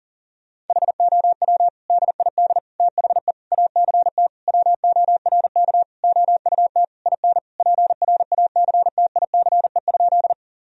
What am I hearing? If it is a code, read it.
Morse code, heard as a keyed tone, about 40 wpm, 700 Hz. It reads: HOW DID THE ACT WORK OUT IN PRACTICE?
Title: Sentences from Top 1000 Words - Review Part 4 - 40wpm